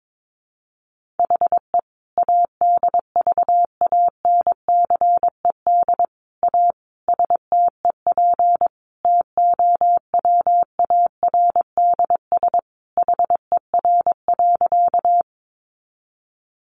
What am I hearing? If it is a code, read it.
Morse code, heard as a keyed tone, about 22 wpm, 700 Hz. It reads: HE ADVANCED A STEP TOWARDS HER.
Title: Sentences from Top 1200 Words 22wpm